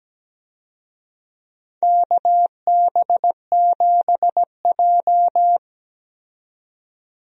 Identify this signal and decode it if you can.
Morse code, heard as a keyed tone, about 17 wpm, 700 Hz. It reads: KB7J
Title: USA Call Signs - 4 Characters - 17wpm